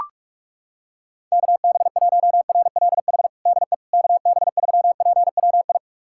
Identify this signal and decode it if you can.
Morse code, heard as a keyed tone, about 42 wpm, 700 Hz. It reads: KB1RLH DE KB3PWI
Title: USA Callsign DE Callsign 45wpm